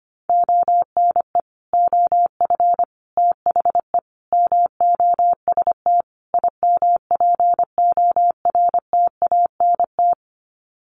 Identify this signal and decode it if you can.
Morse code, heard as a keyed tone, about 25 wpm, 700 Hz. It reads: ONE OF THE MOST IMPORTANT
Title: Sets of 5 Words 25wpm